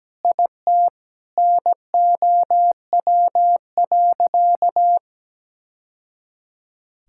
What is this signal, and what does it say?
Morse code, heard as a keyed tone, about 17 wpm, 700 Hz. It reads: IT NOW.
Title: Sentences from Top 1000 Words - Review Part 5 - 17wpm